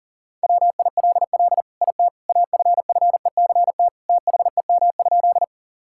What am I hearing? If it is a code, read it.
Morse code, heard as a keyed tone, about 40 wpm, 700 Hz. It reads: WILL IT AFFECT THEM?
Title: Sentences from Top 1000 Words - Review Part 6 - 40wpm